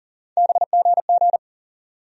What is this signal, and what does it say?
Morse code, heard as a keyed tone, about 40 wpm, 700 Hz. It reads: BGG
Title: USA Call Sign Postfixes - 3 Letters 40wpm